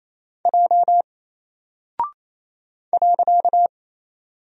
Morse code, heard as a keyed tone, about 27 wpm, 700 Hz. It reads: J .
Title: Single Character (Letter:Number:Punctuation:Prosign) 28wpm